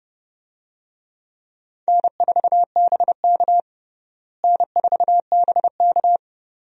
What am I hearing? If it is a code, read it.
Morse code, heard as a keyed tone, about 29 wpm, 700 Hz. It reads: N4BK N4BK
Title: USA Call Signs - 4 Characters - Repeated 2x - 30wpm